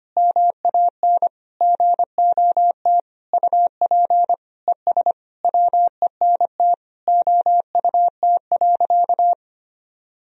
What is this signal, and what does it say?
Morse code, heard as a keyed tone, about 25 wpm, 700 Hz. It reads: MAN GOT UP ES WENT OUT.
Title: Sentences from Top 300 Words and Abbreviations 25wpm